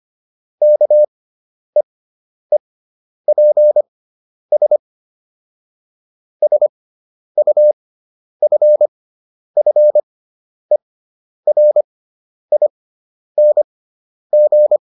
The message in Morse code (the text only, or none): KEEPS SUFFERING